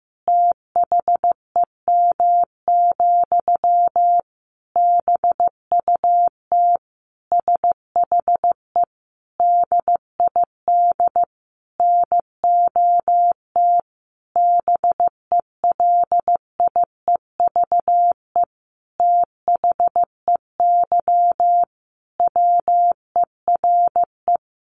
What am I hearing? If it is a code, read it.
Morse code, heard as a keyed tone, about 15 wpm, 700 Hz. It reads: THEM, BUT SHE DID NOT BELIEVE THEY WERE
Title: Sentences from Top 1000 Words 15wpm